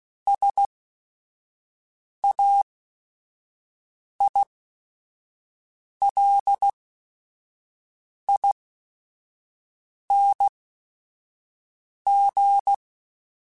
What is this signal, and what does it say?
Morse code, heard as a keyed tone, about 16 wpm, spaced as for 5 wpm, 800 Hz. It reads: SAILING